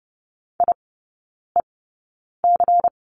Morse code, heard as a keyed tone, about 30 wpm, spaced as for 9 wpm, 700 Hz. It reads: IEC
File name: Lesson 19e - Letter w - Three Characters 30wpm